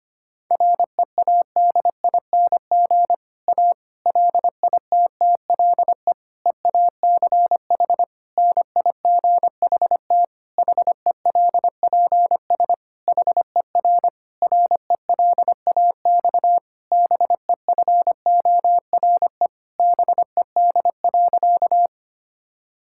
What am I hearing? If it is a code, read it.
Morse code, heard as a keyed tone, about 25 wpm, 700 Hz. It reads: READING A LITTLE EACH NIGHT HELPS HER RELAX BEFORE BED.
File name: practice / 25wpm / Difficult Sentences with Hints - 25wpm